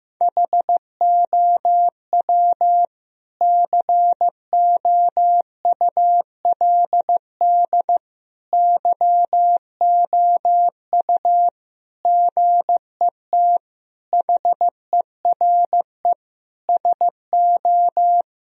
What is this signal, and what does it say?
Morse code, heard as a keyed tone, about 15 wpm, 700 Hz. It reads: HOW COULD YOU GET HERE SO
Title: Sentences from Top 1000 Words - Review Part 2 - 15wpm